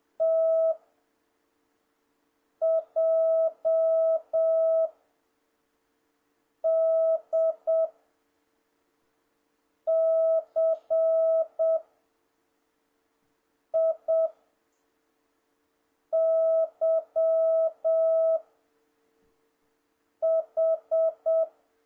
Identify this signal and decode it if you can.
Morse code, heard as a keyed tone, about 7 wpm, 650 Hz. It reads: T J D C I Y H
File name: jh8cbh / 2026cw